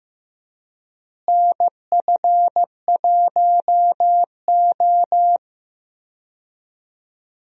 Morse code, heard as a keyed tone, about 15 wpm, 700 Hz. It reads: NF1O
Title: USA Call Signs - 4 Characters - Repeated 2x - 15wpm